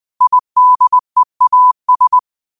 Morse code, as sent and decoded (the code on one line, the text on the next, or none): .. -.. .- ...
IDEAS